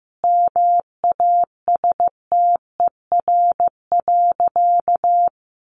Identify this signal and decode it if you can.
Morse code, heard as a keyed tone, about 15 wpm, 700 Hz. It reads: MASTER.
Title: Sentences from Top 1300 Words 15wpm